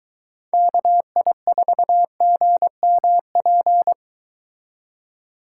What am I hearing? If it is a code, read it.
Morse code, heard as a keyed tone, about 23 wpm, 700 Hz. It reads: KI4GMP